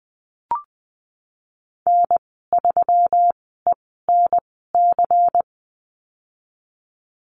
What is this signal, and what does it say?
Morse code, heard as a keyed tone, about 20 wpm, spaced as for 15 wpm, 700 Hz. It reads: N3ENC
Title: 179 Lesson 29g - Slash - Call Signs 20wpm